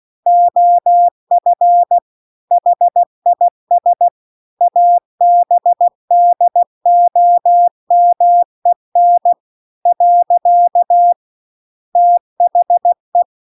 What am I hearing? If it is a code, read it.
Morse code, heard as a keyed tone, about 16 wpm, 700 Hz. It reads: OF HIS ABDOMEN . THE
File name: CW-The-Big-Twoheaded-River-2-16WPM